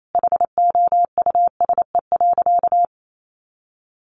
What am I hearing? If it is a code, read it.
Morse code, heard as a keyed tone, about 28 wpm, 700 Hz. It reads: HOUSE.
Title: Sentences from Top 1500 Words 28wpm